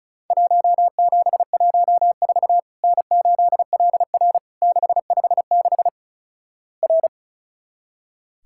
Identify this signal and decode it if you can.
Morse code, heard as a keyed tone, about 36 wpm, 700 Hz. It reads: 1714 N8LR 656 R
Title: Worked All Europe DX Contest - QTC Exchange - 35wpm